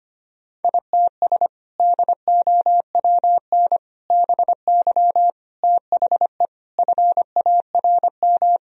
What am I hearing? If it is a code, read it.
Morse code, heard as a keyed tone, about 25 wpm, 700 Hz. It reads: ITS DOWN BY THE FARM
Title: Sentences from Top 1000 Words - Review Part 2 - 25wpm